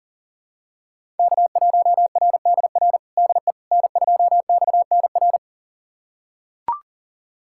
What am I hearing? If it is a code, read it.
Morse code, heard as a keyed tone, about 37 wpm, 700 Hz. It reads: K1RDR DE N2XNR
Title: USA Callsign DE Callsign - 5 Characters - 40wpm